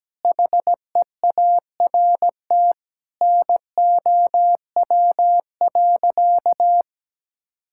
Morse code, heard as a keyed tone, about 17 wpm, 700 Hz. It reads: HEART NOW.
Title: Sentences from Top 1600 Words 17wpm